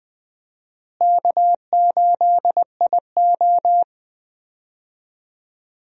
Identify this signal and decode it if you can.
Morse code, heard as a keyed tone, about 20 wpm, 700 Hz. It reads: K8IO
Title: USA Call Signs - 4 Characters - Repeated 2x - 20wpm